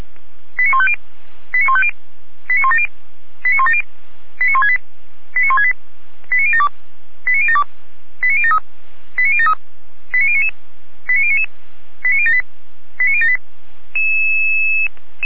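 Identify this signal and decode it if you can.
Five-tone selective call (ZVEI), heard as quick runs of five tones, tones of about 70 ms. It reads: 89150 89150 89170 89170 89168 89168 89072 89072 89073 89073 89099 89099 89078 89078
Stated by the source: ZVEI five-tone selective call 89169, received by SDR.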